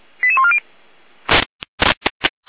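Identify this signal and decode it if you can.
Five-tone selective call (ZVEI), heard as quick runs of five tones, tones of about 70 ms. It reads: 88140
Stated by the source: Five-tone ZVEI selective call received by SDR, fire brigade Feuerwehr Orlamünde.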